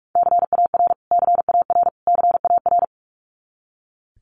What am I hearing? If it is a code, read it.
Morse code, heard as a keyed tone, about 44 wpm, 700 Hz. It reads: CAR CAR CAR
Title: Top 300 Words Repeated 45wpm